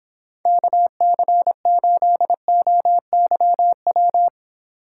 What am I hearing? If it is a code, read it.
Morse code, heard as a keyed tone, about 26 wpm, 700 Hz. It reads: KC8OYW